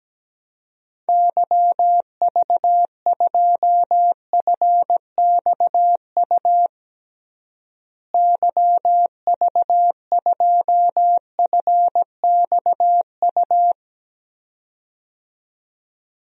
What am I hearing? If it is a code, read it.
Morse code, heard as a keyed tone, about 17 wpm, 700 Hz. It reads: YV2FXU YV2FXU
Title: DXCC-100 Call Signs - 6 Characters - Repeated 2x - 17wpm